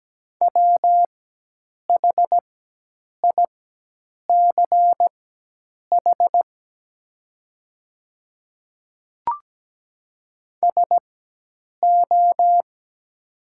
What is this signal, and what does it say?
Morse code, heard as a keyed tone, about 17 wpm, spaced as for 8 wpm, 700 Hz. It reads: WHICH SO
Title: 4x Character Spacing - Top 100 Words - 17wpm